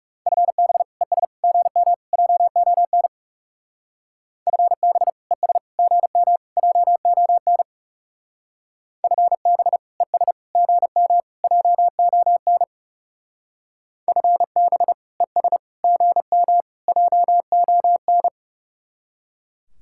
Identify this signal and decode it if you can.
Morse code, heard as a keyed tone, about 38 wpm, 700 Hz. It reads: FB ES GM JON FB ES GM JON FB ES GM JON FB ES GM JON